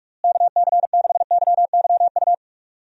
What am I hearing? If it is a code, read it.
Morse code, heard as a keyed tone, about 46 wpm, 700 Hz. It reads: KC6YYU
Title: USA Call Signs - 6 Characters - Repeated 2x - 45wpm